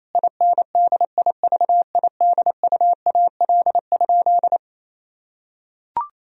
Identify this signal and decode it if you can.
Morse code, heard as a keyed tone, about 28 wpm, 700 Hz. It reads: INDIVIDUAL?
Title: Sentences from Top 1000 Words - Review Part 4 - 28wpm